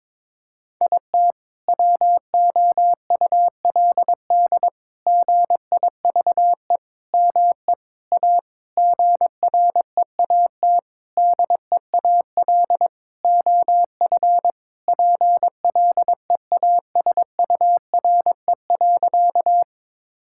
Morse code, heard as a keyed tone, about 22 wpm, 700 Hz. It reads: IT WOULD GIVE ME A GREAT DEAL OF PLEASURE.
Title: Sentences from Top 1600 Words 22wpm